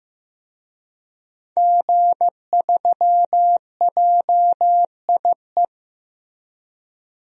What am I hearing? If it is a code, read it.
Morse code, heard as a keyed tone, about 15 wpm, 700 Hz. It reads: G3JIE
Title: DXCC-100 Call Signs - 5 Characters - Repeated 3x - 15wpm